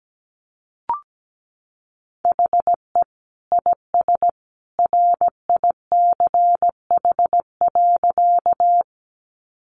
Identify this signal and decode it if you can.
Morse code, heard as a keyed tone, about 17 wpm, 700 Hz. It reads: HE IS RICH.
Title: Sentences from Top 1100 Words 17wpm